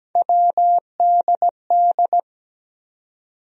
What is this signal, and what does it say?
Morse code, heard as a keyed tone, about 17 wpm, 700 Hz. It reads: WDD
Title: USA Call Sign Postfixes - 3 Letters 17wpm